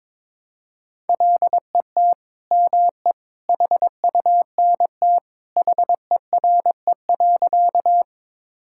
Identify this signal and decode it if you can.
Morse code, heard as a keyed tone, about 22 wpm, 700 Hz. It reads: LET ME HUNT HERE.
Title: Sentences from Top 1600 Words 22wpm